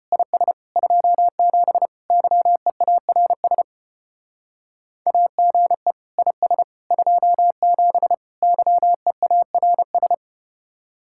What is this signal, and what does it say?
Morse code, heard as a keyed tone, about 34 wpm, 700 Hz. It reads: IS 27 YEARS AGE IS 27 YEARS